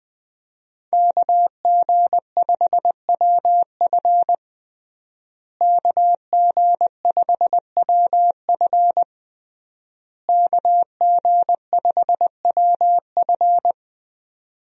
Morse code, heard as a keyed tone, about 19 wpm, 700 Hz. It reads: KG5WF KG5WF KG5WF
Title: USA Call Signs - 5 Characters - Repeated 3x - 20wpm